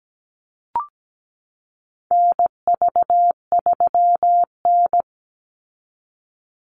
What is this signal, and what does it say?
Morse code, heard as a keyed tone, about 17 wpm, 700 Hz. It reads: NV3N
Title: Call Signs Repeated 2x - 17wpm